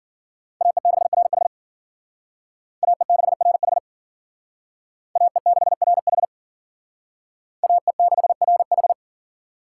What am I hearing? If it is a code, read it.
Morse code, heard as a keyed tone, about 52 wpm, 700 Hz. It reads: AE6RH AE6RH AE6RH AE6RH